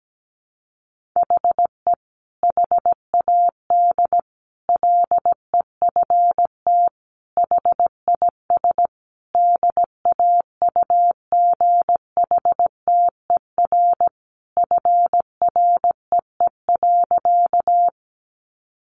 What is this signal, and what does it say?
Morse code, heard as a keyed tone, about 17 wpm, 700 Hz. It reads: HE HAD LEFT HIS DAUGHTER FREE.
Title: Sentences from Top 1500 Words 17wpm